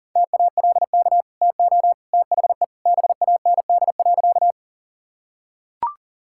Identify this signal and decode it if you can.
Morse code, heard as a keyed tone, about 40 wpm, 700 Hz. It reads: TALK TO THE BAND.
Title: Sentences from Top 1600 Words 40wpm